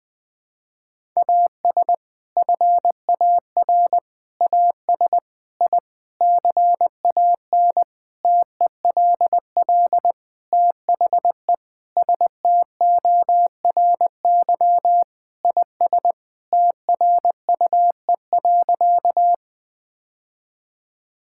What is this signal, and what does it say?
Morse code, heard as a keyed tone, about 21 wpm, 700 Hz. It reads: AS FAR AS I CAN TELL THE STORY IS TRUE.
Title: Sentences with 7-Word Phrases - Spoken Beginning - 20wpm